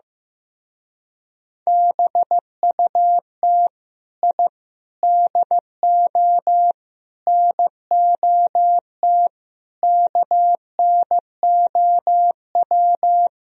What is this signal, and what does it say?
Morse code, heard as a keyed tone, about 16 wpm, 700 Hz. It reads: BUT I DO NOT KNOW
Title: Speed-Racing - Sets of 5 Words 15wpm